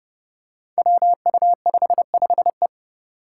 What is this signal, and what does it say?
Morse code, heard as a keyed tone, about 30 wpm, 700 Hz. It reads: WU55E
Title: Five Challenging Letter-Number Combinations 30wpm